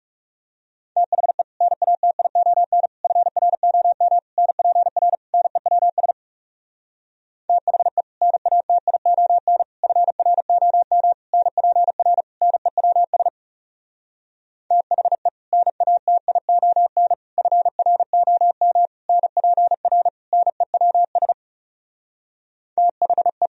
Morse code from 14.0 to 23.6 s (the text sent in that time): THE NATION FROM NPR NEWS THE